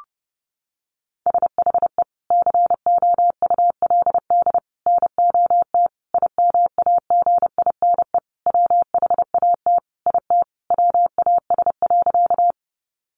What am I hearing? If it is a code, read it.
Morse code, heard as a keyed tone, about 30 wpm, 700 Hz. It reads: SHE COULD NOT IMAGINE WHAT IT WAS.
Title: Sentences from Top 1400 Words 30wpm